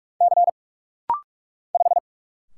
Morse code, heard as a keyed tone, about 45 wpm, 700 Hz. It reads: C 5
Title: Single Letter-Number Alt - Rapid-Fire 45wpm